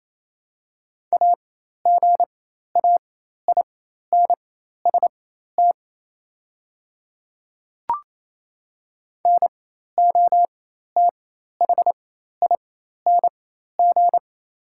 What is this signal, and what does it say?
Morse code, heard as a keyed tone, about 27 wpm, spaced as for 13 wpm, 700 Hz. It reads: AGAINST NOTHING
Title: 4x Character Spacing - Top 500 Words - 28wpm